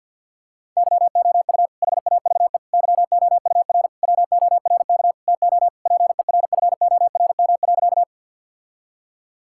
Morse code, heard as a keyed tone, about 50 wpm, 700 Hz. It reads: YOU HAVE YOUR WORK TO PERFORM.
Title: Sentences from Top 1000 Words - Review Part 5 - 50wpm